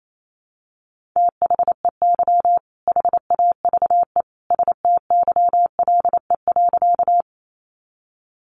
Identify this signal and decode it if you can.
Morse code, heard as a keyed tone, about 28 wpm, 700 Hz. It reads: THEY HAVE STYLE.
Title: Sentences from Top 800 Words 28wpm